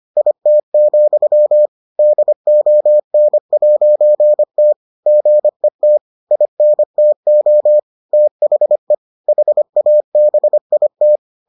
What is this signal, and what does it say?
Morse code, heard as a keyed tone, about 25 wpm, 600 Hz. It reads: IT, DON'T GET INTO THE HABIT